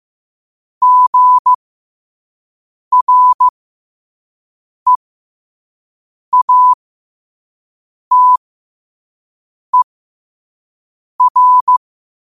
Morse code, heard as a keyed tone, about 15 wpm, spaced as for 5 wpm, 1000 Hz. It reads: GREATER
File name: qotdfiles / 5wpm